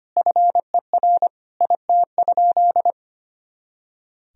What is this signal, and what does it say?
Morse code, heard as a keyed tone, about 25 wpm, 700 Hz. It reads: FER IT?
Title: Sentences from Top 400 Words and Abbreviations 25wpm